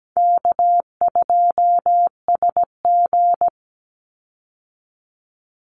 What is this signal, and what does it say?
Morse code, heard as a keyed tone, about 17 wpm, 700 Hz. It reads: K2SG
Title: Easy Call Signs Repeated 3x - 17wpm